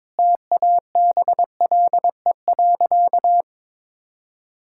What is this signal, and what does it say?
Morse code, heard as a keyed tone, about 22 wpm, 700 Hz. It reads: TABLE.
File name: practice / 22wpm / Easy Sentences with Hints - 22wpm